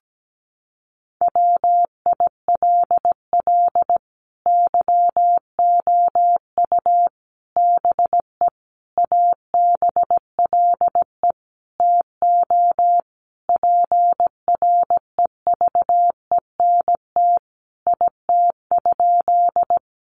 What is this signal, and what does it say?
Morse code, heard as a keyed tone, about 17 wpm, 700 Hz. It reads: WILL YOU BE ABLE TO PREVENT IT?
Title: Sentences from Top 1100 Words 17wpm